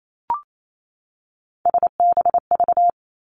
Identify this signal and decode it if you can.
Morse code, heard as a keyed tone, about 28 wpm, 700 Hz. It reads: SBV
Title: Three Challenging Letter-Number Combinations 28wpm